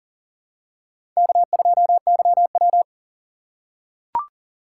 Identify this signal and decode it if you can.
Morse code, heard as a keyed tone, about 41 wpm, 700 Hz. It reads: K2YW
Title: USA Call Signs - 4 Characters - 3x Character Spacing - 40wpm